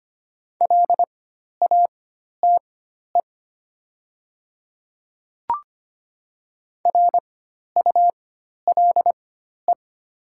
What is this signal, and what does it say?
Morse code, heard as a keyed tone, about 25 wpm, spaced as for 12 wpm, 700 Hz. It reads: LATE RULE
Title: 4x Character Spacing - Top 500 Words - 25wpm